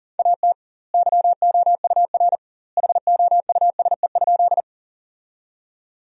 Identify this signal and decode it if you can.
Morse code, heard as a keyed tone, about 40 wpm, 700 Hz. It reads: AT YOUR HOUSE?
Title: 2x Word-Spacing - Sentences from Top 200 Words - 40wpm